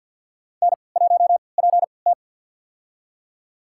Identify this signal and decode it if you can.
Morse code, heard as a keyed tone, about 52 wpm, spaced as for 29 wpm, 700 Hz. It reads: N1PT
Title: USA Call Signs - 4 Characters - 3x Character Spacing - 50wpm